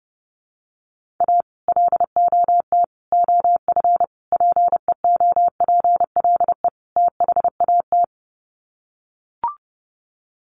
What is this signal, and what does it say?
Morse code, heard as a keyed tone, about 30 wpm, 700 Hz. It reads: A LOT OF PEOPLE THAT
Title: Sets of 5 Words Encore2 30wpm